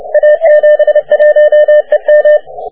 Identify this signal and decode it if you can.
Morse code, heard as a keyed tone, about 30 wpm, 600 Hz. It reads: A71EM
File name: wp / a71em 20m cw